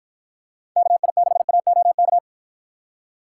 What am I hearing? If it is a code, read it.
Morse code, heard as a keyed tone, about 52 wpm, 700 Hz. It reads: KI6AOK